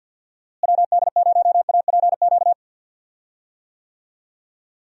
Morse code, heard as a keyed tone, about 52 wpm, 700 Hz. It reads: WD0APQ
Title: USA Call Signs - 6 Characters - Repeated 3x - 50wpm